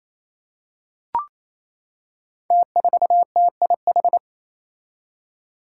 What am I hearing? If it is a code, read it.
Morse code, heard as a keyed tone, about 28 wpm, 700 Hz. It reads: T4TIH
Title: Five Challenging Letter-Number Combinations 28wpm